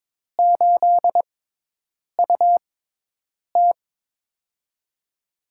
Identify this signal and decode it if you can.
Morse code, heard as a keyed tone, about 22 wpm, spaced as for 8 wpm, 700 Hz. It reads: 8UT
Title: USA Call Signs - 6x Character Spacing - 22wpm